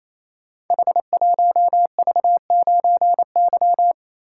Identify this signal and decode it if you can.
Morse code, heard as a keyed tone, about 27 wpm, 700 Hz. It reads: H1V9Y
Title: Code Groups - 5 Letter:Numbers - 28wpm